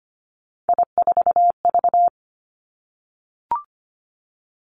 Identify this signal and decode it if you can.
Morse code, heard as a keyed tone, about 25 wpm, 700 Hz. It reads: I4V
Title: Three Challenging Letter-Number Combinations 25wpm